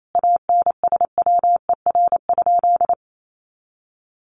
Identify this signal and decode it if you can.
Morse code, heard as a keyed tone, about 28 wpm, 700 Hz. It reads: ANSWER?
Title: Sentences from Top 1100 Words 28wpm